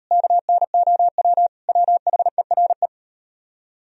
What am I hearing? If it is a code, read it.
Morse code, heard as a keyed tone, about 36 wpm, 700 Hz. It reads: KNOW WHERE